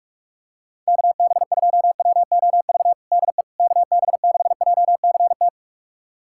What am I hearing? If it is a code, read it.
Morse code, heard as a keyed tone, about 45 wpm, 690 Hz. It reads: KB2WOV DE KB6JCT
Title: USA Callsign DE Callsign - 6 Characters - 45wpm